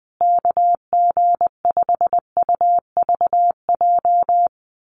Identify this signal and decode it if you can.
Morse code, heard as a keyed tone, about 20 wpm, 700 Hz. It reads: KG5UVJ
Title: USA Call Signs Repeated 3x 20wpm